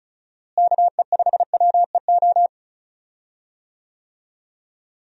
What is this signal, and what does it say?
Morse code, heard as a keyed tone, about 36 wpm, 700 Hz. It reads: KE5WEO
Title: USA Call Signs - 6 Characters - Repeated 2x - 35wpm